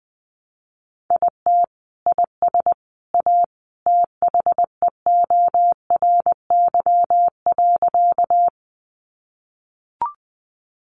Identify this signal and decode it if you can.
Morse code, heard as a keyed tone, about 20 wpm, 700 Hz. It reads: IT IS A THEORY.
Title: Sentences from Top 600 Words 20wpm